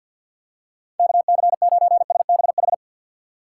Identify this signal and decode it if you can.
Morse code, heard as a keyed tone, about 52 wpm, 700 Hz. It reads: KC9SBH
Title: USA Call Signs - 6 Characters - 3x Character Spacing - 50wpm